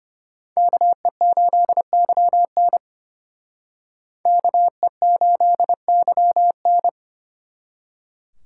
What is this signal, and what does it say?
Morse code, heard as a keyed tone, about 32 wpm, 700 Hz. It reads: KE8YN KE8YN